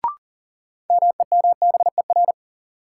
Morse code, heard as a keyed tone, about 37 wpm, 700 Hz. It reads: MEMBER
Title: Sets of 3 Words - New Word Drill - 40wpm